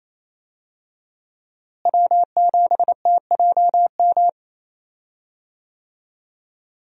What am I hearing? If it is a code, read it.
Morse code, heard as a keyed tone, about 28 wpm, 700 Hz. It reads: W7TJM
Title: USA Call Signs - 5 Characters - Repeated 3x - 28wpm